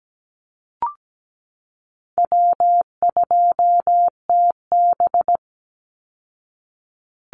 Morse code, heard as a keyed tone, about 17 wpm, 700 Hz. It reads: W2TB
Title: Call Signs Repeated 3x - 17wpm